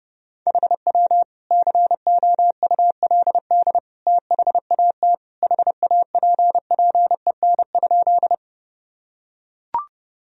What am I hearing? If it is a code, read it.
Morse code, heard as a keyed tone, about 30 wpm, 700 Hz. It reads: HW COULD THAT HAPPEN?
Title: Sentences from Top 400 Words and Abbreviations 30wpm